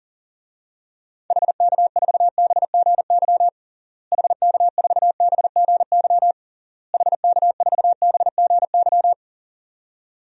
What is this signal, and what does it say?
Morse code, heard as a keyed tone, about 37 wpm, 700 Hz. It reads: HK4BGY HK4BGY HK4BGY
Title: DXCC-100 Call Signs - 6 Characters - Repeated 3x - 40wpm